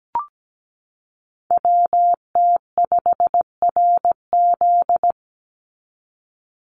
Morse code, heard as a keyed tone, about 17 wpm, 700 Hz. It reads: WT5RZ
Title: Call Signs Repeated 3x - 17wpm